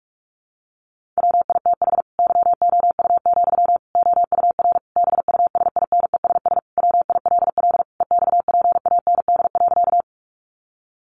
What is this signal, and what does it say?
Morse code, heard as a keyed tone, about 44 wpm, 700 Hz. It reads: WITH YOU, OUR BUSINESS WILL EXPAND.
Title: Sentences from Top 1400 Words 45wpm